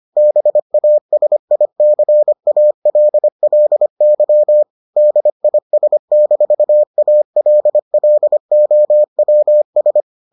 Morse code, heard as a keyed tone, about 25 wpm, 600 Hz. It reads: BASICALLY DIS-ALLOWS